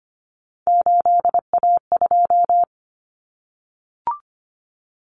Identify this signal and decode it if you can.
Morse code, heard as a keyed tone, about 25 wpm, 700 Hz. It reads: 8A2
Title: Common Cut Numbers - 3-Digit 25wpm